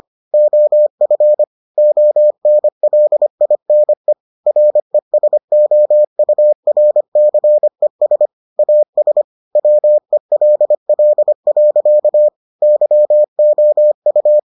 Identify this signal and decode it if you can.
Morse code, heard as a keyed tone, about 25 wpm, 600 Hz. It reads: OF ONLINE RESOURCES AS WELL. YOU